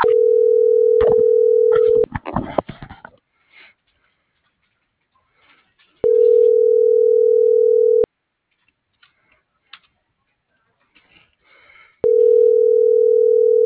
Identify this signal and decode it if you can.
Ringback tone (North America), 440 + 480 Hz.